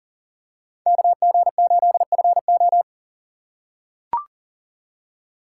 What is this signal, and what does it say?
Morse code, heard as a keyed tone, about 40 wpm, 700 Hz. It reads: KG8FO
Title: USA Call Signs - 5 Characters - 3x Character Spacing - 40wpm